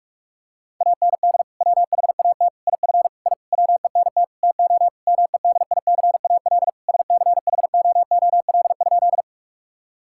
Morse code, heard as a keyed tone, about 46 wpm, 700 Hz. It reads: AND WHAT IF I WENT TO MEDICAL SCHOOL?
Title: Sentences from Top 1000 Words - Review Part 5 - 45wpm